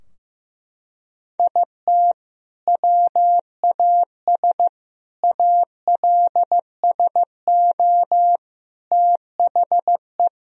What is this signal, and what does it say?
Morse code, heard as a keyed tone, about 15 wpm, 700 Hz. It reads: IT WAS ALSO THE
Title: Sentences from Top 1000 Words - Review Part 2 - 15wpm